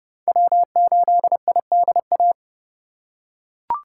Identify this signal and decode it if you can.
Morse code, heard as a keyed tone, about 28 wpm, 700 Hz. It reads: W8IDA